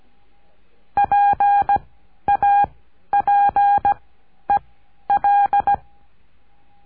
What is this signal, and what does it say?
Morse code, heard as a keyed tone, about 17 wpm, 810 Hz. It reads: P A P E L
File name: cursos / CURSO DE TELEGRAFIA / Telegrafia - Trilha 14